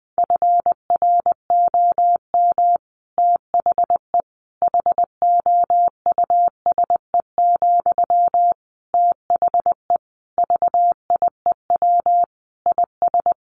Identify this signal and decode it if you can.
Morse code, heard as a keyed tone, about 20 wpm, 700 Hz. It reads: FROM THE HOUSE, THE VIEW IS
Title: Sentences from Top 1100 Words 20wpm